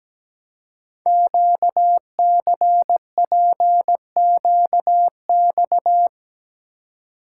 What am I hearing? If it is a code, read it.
Morse code, heard as a keyed tone, about 17 wpm, 700 Hz. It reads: QCPQX
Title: Code Groups - 5 Letters - 17wpm